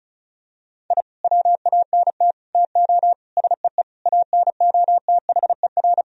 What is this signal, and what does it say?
Morse code, heard as a keyed tone, about 34 wpm, 700 Hz. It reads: I WANT TO SEE ANOTHER
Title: Sentences from Top 1000 Words - Review Part 1 - 35wpm